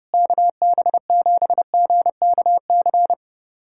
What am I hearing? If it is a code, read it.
Morse code, heard as a keyed tone, about 30 wpm, 700 Hz. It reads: KB7GKC
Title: USA Callsign DE Callsign - 6 Characters - 30wpm